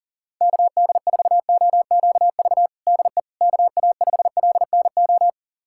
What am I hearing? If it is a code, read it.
Morse code, heard as a keyed tone, about 40 wpm, 700 Hz. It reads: KD4OQV DE KA5LNO
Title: USA Callsign DE Callsign - 6 Characters - 40wpm